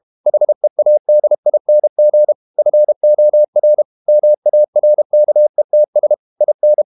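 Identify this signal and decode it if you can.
Morse code, heard as a keyed tone, about 32 wpm, 600 Hz. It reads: HEADING FOR MARKETS IN
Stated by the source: Morse code practice at 32 wpm from CW med Gnister, reading Ken Follett's novel Never in English.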